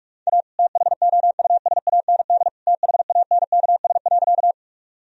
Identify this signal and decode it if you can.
Morse code, heard as a keyed tone, about 46 wpm, 700 Hz. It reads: A THOUSAND THANKS.
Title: Sentences from Top 1000 Words - Review Part 3 - 45wpm